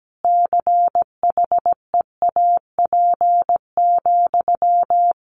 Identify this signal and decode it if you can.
Morse code, heard as a keyed tone, about 17 wpm, 700 Hz. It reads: CHEAP,
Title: Sentences from Top 1200 Words 17wpm